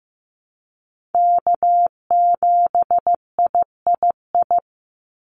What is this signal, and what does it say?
Morse code, heard as a keyed tone, about 15 wpm, 700 Hz. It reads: K7III
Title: NA SOTA Chaser Callsigns 15wpm